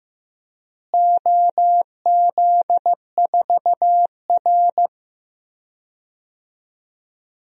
Morse code, heard as a keyed tone, about 16 wpm, 700 Hz. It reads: OZ4R
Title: DXCC-100 Call Signs - 4 Characters - Repeated 2x - 15wpm